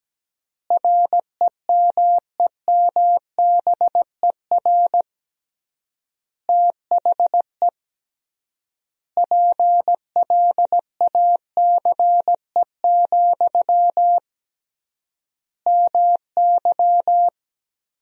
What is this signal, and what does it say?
Morse code, heard as a keyed tone, about 17 wpm, 700 Hz. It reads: REMEMBER THE PLACE, MY
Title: 3x Word-Spacing - Sentences from Top 400 Words - 17wpm